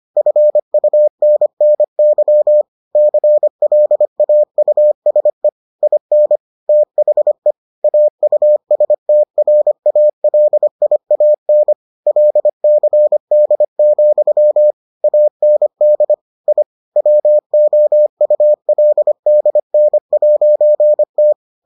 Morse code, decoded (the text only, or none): FUNNY CLAUSE IN THE AUSTRALIAN LCD, AND I WOULDN'T